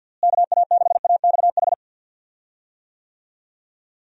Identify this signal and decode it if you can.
Morse code, heard as a keyed tone, about 51 wpm, 700 Hz. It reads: KA6AXH
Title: USA Call Signs - 6 Characters - Repeated 3x - 50wpm